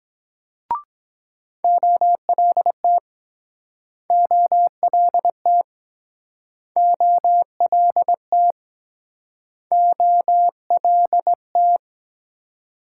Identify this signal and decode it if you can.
Morse code, heard as a keyed tone, about 22 wpm, 700 Hz. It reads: OLT OLT OLT OLT